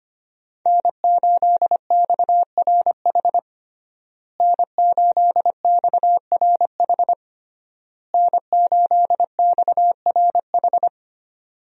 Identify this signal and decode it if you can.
Morse code, heard as a keyed tone, about 25 wpm, 700 Hz. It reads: N8XRH N8XRH N8XRH
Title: USA Call Signs - 5 Characters - Repeated 3x - 25wpm